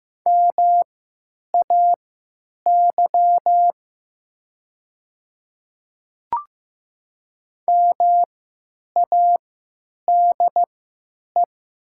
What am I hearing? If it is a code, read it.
Morse code, heard as a keyed tone, about 16 wpm, spaced as for 9 wpm, 700 Hz. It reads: MAY MADE
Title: 3x Character Spacing - Top 300 Words - 15wpm